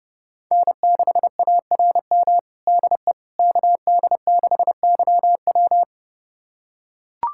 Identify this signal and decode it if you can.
Morse code, heard as a keyed tone, about 30 wpm, 700 Hz. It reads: N6ARM DE KD6YW
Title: USA Callsign DE Callsign - 5 Characters - 30wpm